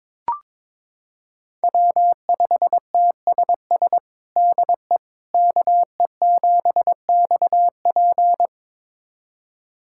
Morse code, heard as a keyed tone, about 22 wpm, 700 Hz. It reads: W5TSS DE KE7XP
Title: USA Callsign DE Callsign 22wpm